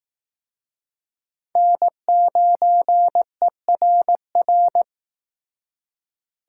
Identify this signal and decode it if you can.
Morse code, heard as a keyed tone, about 19 wpm, 700 Hz. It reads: N9ERR